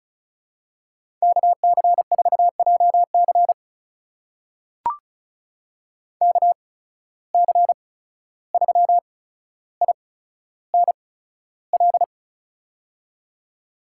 Morse code, heard as a keyed tone, about 36 wpm, spaced as for 14 wpm, 700 Hz. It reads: KC4JC KC3INL
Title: USA Call Signs - 8x Character Spacing - 35wpm